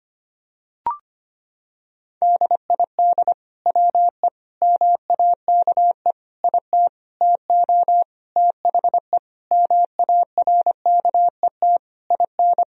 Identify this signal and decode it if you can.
Morse code, heard as a keyed tone, about 25 wpm, 700 Hz. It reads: DID WE MAKE IT TO THE MARKET IN
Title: Sentences from Top 1000 Words - Review Part 3 - 25wpm